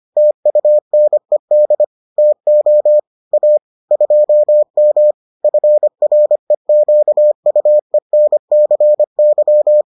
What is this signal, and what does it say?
Morse code, heard as a keyed tone, about 25 wpm, 600 Hz. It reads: TUNED TO A 2M FREQUENCY